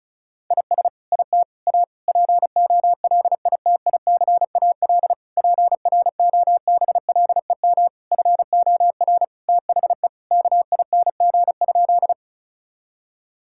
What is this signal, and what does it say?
Morse code, heard as a keyed tone, about 34 wpm, 700 Hz. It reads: IS IT A POLITICAL PROBLEM FOR THE KING?
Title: Sentences from Top 1000 Words - Review Part 5 - 35wpm